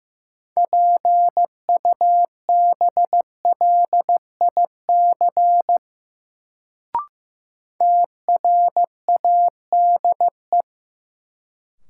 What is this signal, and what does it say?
Morse code, heard as a keyed tone, about 15 wpm, 700 Hz. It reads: PUBLIC TRADE
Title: Sentences from Top 600 Words - New Word Drill - 15wpm